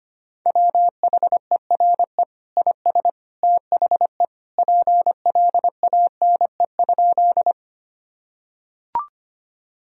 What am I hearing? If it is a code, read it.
Morse code, heard as a keyed tone, about 25 wpm, 700 Hz. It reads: WHERE IS THE PLANE?
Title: Sentences from Top 1000 Words - Review Part 3 - 25wpm